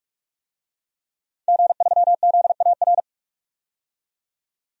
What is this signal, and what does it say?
Morse code, heard as a keyed tone, about 45 wpm, 700 Hz. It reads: G3ZAR
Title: DXCC-100 Call Signs - 5 Characters - 45wpm